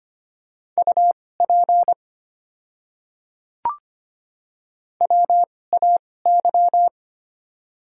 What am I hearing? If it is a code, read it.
Morse code, heard as a keyed tone, about 25 wpm, spaced as for 18 wpm, 700 Hz. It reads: UP WAY
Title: 2x Character Spacing - Top 100 Words - 25wpm